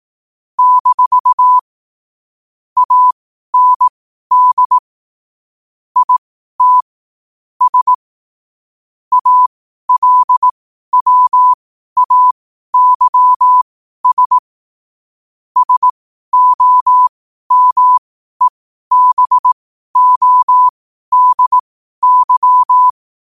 Morse code, heard as a keyed tone, about 18 wpm, spaced as for 12 wpm, 1000 Hz. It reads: - AND IT S ALWAYS SOMEBODY
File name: qotdfiles / 12wpm